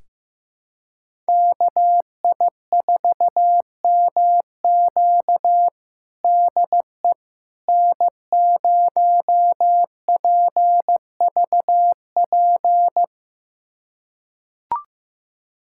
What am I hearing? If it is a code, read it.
Morse code, heard as a keyed tone, about 16 wpm, 700 Hz. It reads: KI4MQ DE N0PVP
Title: USA Callsign DE Callsign - 5 Characters - 15wpm